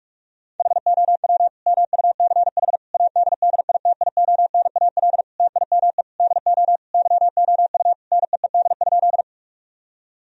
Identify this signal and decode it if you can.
Morse code, heard as a keyed tone, about 48 wpm, 700 Hz. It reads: HOW MUCH ADDITIONAL TIME DO YOU NEED?
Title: Sentences from Top 1000 Words - Review Part 4 - 45wpm